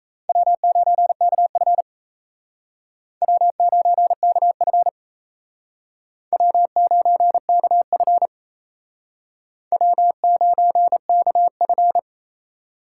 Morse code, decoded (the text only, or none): W9KF W9KF W9KF W9KF